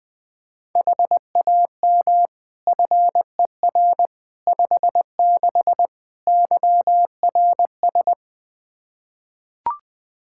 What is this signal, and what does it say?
Morse code, heard as a keyed tone, about 20 wpm, 700 Hz. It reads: HAM FER 56 YRS
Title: Practice QSO - Exchange 3 - More Info - 20wpm